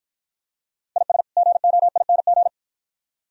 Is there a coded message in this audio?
Yes: Morse code from 1.0 to 2.5 s, heard as a keyed tone, about 51 wpm, 700 Hz. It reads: IS GOING